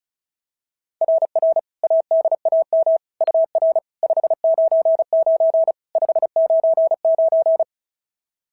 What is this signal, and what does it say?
Morse code, heard as a keyed tone, about 35 wpm, 690 Hz. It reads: RR ADAM UR 599 599
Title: SOTA QSO Practice 35wpm